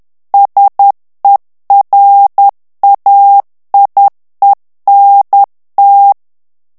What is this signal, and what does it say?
Morse code, heard as a keyed tone, about 10 wpm, 800 Hz. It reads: SERAIENT
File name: CW / FICHIERS-MP3 / FRANCAIS TEXTE CLAIR 2 VITESSE 10 MOTS MINUTES